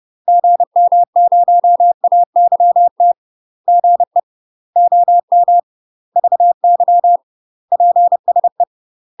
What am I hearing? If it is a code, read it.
Morse code, heard as a keyed tone, about 30 wpm, 700 Hz. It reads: GM0AYT GE OM VY PSE
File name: UK MASTER QSOs TRAINING / 30WPM / UK MASTER QSOs 0028